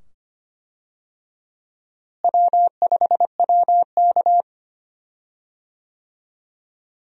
Morse code, heard as a keyed tone, about 25 wpm, 700 Hz. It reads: W5WK
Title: USA Call Signs - 4 Characters - Repeated 2x - 25wpm